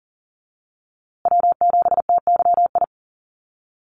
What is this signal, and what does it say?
Morse code, heard as a keyed tone, about 40 wpm, 700 Hz. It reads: W7TYI
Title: USA Call Signs Repeated 2x 40wpm